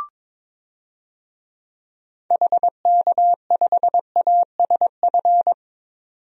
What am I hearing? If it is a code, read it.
Morse code, heard as a keyed tone, about 22 wpm, 700 Hz. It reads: HK5ASF
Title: DXCC-100 Call Signs - 6 Characters - 22wpm